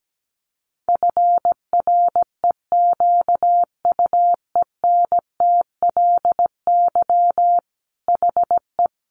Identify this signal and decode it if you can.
Morse code, heard as a keyed tone, about 17 wpm, 700 Hz. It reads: FREQUENTLY HE
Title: Sentences from Top 1300 Words 17wpm